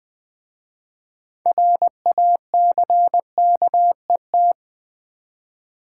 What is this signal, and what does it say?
Morse code, heard as a keyed tone, about 20 wpm, 700 Hz. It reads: RACKET
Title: Easy Words with Hints - 20wpm